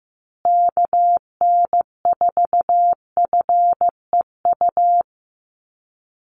Morse code, heard as a keyed tone, about 15 wpm, 700 Hz. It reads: KN4FEU
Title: USA Call Signs Repeated 2x 15wpm